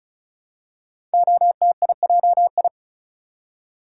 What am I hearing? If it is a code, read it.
Morse code, heard as a keyed tone, about 36 wpm, 700 Hz. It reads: OTIJI
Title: Code Groups - 5 Letters - 35wpm